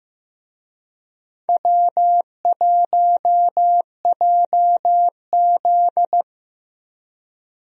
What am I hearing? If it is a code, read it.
Morse code, heard as a keyed tone, about 15 wpm, 700 Hz. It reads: W1JZ
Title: USA Call Signs - 4 Characters - Repeated 3x - 15wpm